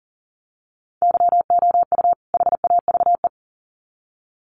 Morse code, heard as a keyed tone, about 40 wpm, 700 Hz. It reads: YOU HAVE
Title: Sets of 2 Words 40wpm